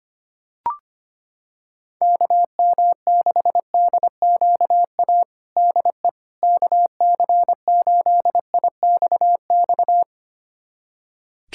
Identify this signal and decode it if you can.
Morse code, heard as a keyed tone, about 25 wpm, 700 Hz. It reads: KM6DQA DE KC8IXX
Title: USA Callsign DE Callsign 25wpm